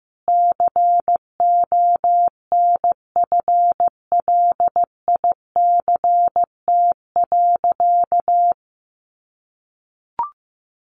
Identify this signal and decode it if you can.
Morse code, heard as a keyed tone, about 15 wpm, 700 Hz. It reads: CONFLICT.
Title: Sentences from Top 1300 Words 15wpm